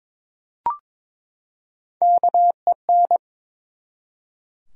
Morse code, heard as a keyed tone, about 22 wpm, 700 Hz. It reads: KEN
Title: USA Common Names - Max 3 Letters - 22wpm